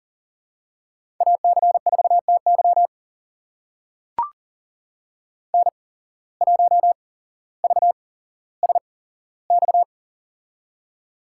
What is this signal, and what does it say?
Morse code, heard as a keyed tone, about 40 wpm, spaced as for 17 wpm, 700 Hz. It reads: AC4TY N1VSX
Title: USA Call Signs - 5 Characters - 8x Character Spacing - 40wpm